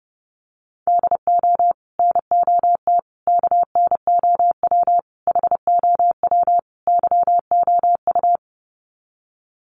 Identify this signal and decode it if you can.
Morse code, heard as a keyed tone, about 30 wpm, 700 Hz. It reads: DO NOT KNOW HOW YOU
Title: Sets of 5 Words Encore 30wpm